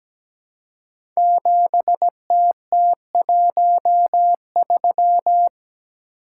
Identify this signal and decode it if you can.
Morse code, heard as a keyed tone, about 17 wpm, 700 Hz. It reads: 7TT13
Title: Code Groups - 5 Letter:Numbers - 17wpm